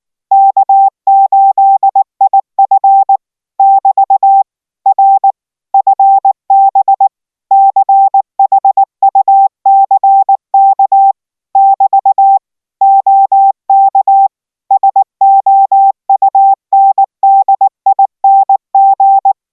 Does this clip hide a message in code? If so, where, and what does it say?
Yes: Morse code from 0.3 to 19.3 s, heard as a keyed tone, about 19 wpm, 780 Hz. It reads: K8IF <BT> R FB CHUCK <BT> OK SOUNDING